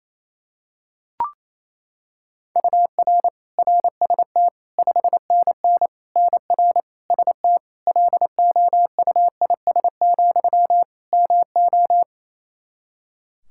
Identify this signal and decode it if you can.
Morse code, heard as a keyed tone, about 28 wpm, 700 Hz. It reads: UR RST 5NN NR ST LOUIS, MO
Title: Practice QSO - Exchange 1 - Beginning - 28wpm